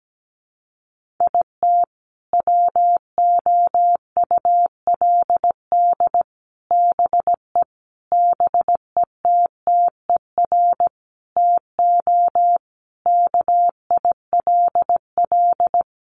Morse code, heard as a keyed tone, about 17 wpm, 700 Hz. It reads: IT WOULD BE BETTER TO KILL
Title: Sentences from Top 1500 Words 17wpm